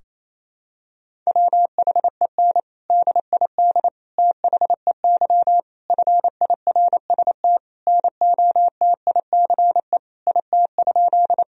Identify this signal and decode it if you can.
Morse code, heard as a keyed tone, about 28 wpm, 700 Hz. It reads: WHEN DID THEY FIRST NOTICE IT?
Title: Sentences from Top 1000 Words - Review Part 2 - 28wpm